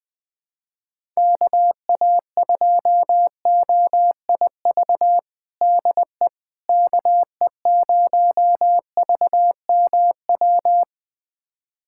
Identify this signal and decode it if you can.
Morse code, heard as a keyed tone, about 21 wpm, 700 Hz. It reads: KA2OIV DE KE0VMW
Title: USA Callsign DE Callsign - 6 Characters - 20wpm